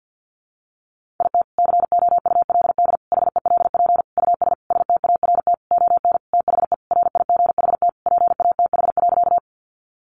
Morse code, heard as a keyed tone, about 50 wpm, 700 Hz. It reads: IT COULD HELP US START ON THE RIGHT PATH.
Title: Sentences from Top 1400 Words 50wpm